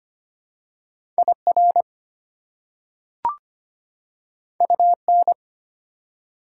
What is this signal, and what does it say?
Morse code, heard as a keyed tone, about 25 wpm, 700 Hz. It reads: IR UN
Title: Prefixes and Suffixes 25wpm